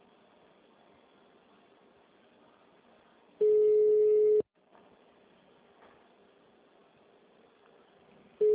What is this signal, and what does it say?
Ringback tone (Europe, 425 Hz).